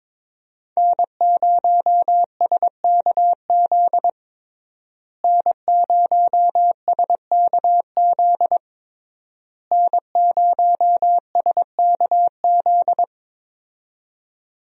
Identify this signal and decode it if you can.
Morse code, heard as a keyed tone, about 22 wpm, 700 Hz. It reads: N0SKZ N0SKZ N0SKZ
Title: USA Call Signs - 5 Characters - Repeated 3x - 22wpm